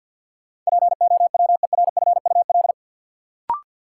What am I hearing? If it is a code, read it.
Morse code, heard as a keyed tone, about 53 wpm, 700 Hz. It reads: POWERFUL
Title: Sentences from Top 1000 Words - Review Part 6 - New Word Drill - 50wpm